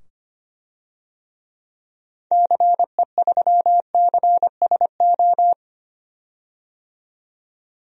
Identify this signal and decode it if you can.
Morse code, heard as a keyed tone, about 25 wpm, 700 Hz. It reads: CE3CSO
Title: DXCC-100 Call Signs - 6 Characters - 25wpm